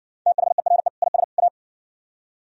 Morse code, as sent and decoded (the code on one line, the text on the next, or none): - .... .-. / .. ... / .-
THERE IS A